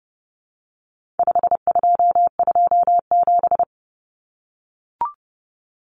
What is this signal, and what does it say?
Morse code, heard as a keyed tone, about 30 wpm, 700 Hz. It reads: 5227
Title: Common Cut Numbers - 4-Digit 30wpm